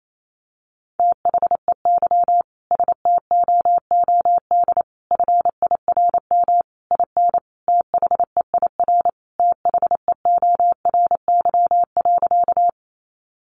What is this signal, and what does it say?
Morse code, heard as a keyed tone, about 28 wpm, 700 Hz. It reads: THEY STOOD FIRM IN THEIR THEORY.
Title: Sentences from Top 1400 Words 28wpm